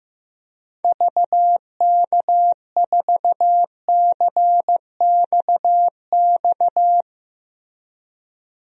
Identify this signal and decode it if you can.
Morse code, heard as a keyed tone, about 15 wpm, 700 Hz. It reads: VK4CXX
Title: DXCC-100 Call Signs - 6 Characters - 15wpm